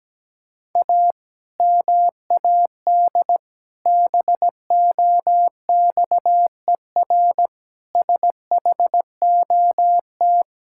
Morse code, heard as a keyed tone, about 17 wpm, 700 Hz. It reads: A MAD BOXER SHOT
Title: Pangram Challenge v2 - 17wpm